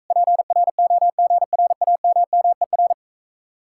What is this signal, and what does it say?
Morse code, heard as a keyed tone, about 46 wpm, 700 Hz. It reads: PROGRAMMER